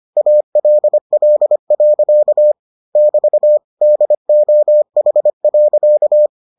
Morse code, heard as a keyed tone, about 25 wpm, 600 Hz. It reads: ALL. <BT> DOH.